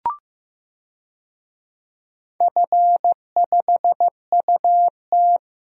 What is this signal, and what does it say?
Morse code, heard as a keyed tone, about 15 wpm, 700 Hz. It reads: F5UT
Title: DXCC-100 Call Signs - 4 Characters - 15wpm